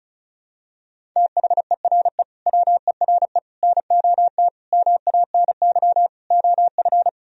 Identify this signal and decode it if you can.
Morse code, heard as a keyed tone, about 34 wpm, 700 Hz. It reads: THERE WERE NOT MANY OF
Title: Sentences from Top 1000 Words - Review Part 1 - 35wpm